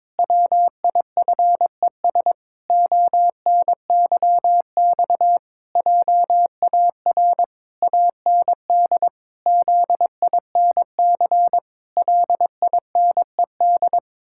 Morse code, heard as a keyed tone, about 22 wpm, 700 Hz. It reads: WIFES ONYX JAR AND ZINC LINED
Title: Pangram Challenge v2 - 22wpm